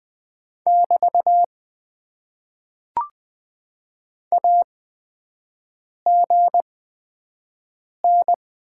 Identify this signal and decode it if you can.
Morse code, heard as a keyed tone, about 20 wpm, spaced as for 5 wpm, 700 Hz. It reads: <BT> AGN
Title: Top 90 QSO Elements - 8x Character Spacing - 20wpm